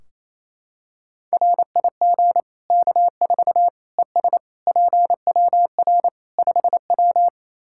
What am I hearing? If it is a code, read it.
Morse code, heard as a keyed tone, about 28 wpm, 700 Hz. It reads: RIG K4 ES PWR 5W
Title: K1USN SST QSO Practice 28wpm